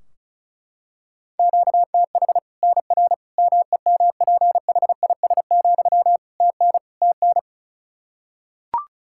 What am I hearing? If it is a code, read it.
Morse code, heard as a keyed tone, about 36 wpm, 700 Hz. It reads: QTH NR MEMPHIS, TN TN
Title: Practice QSO - Exchange 1 - Beginning - 35wpm